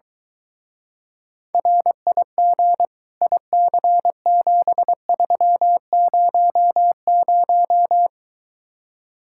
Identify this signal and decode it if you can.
Morse code, heard as a keyed tone, about 22 wpm, 700 Hz. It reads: RIG IC7300